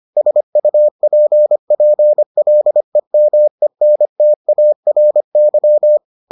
Morse code, heard as a keyed tone, about 25 wpm, 600 Hz. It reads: SUPPLEMENTARY